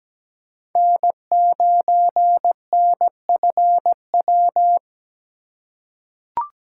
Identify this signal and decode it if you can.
Morse code, heard as a keyed tone, about 17 wpm, 700 Hz. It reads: N9NFW
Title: USA Call Signs - 5 Characters - 3x Character Spacing - 17wpm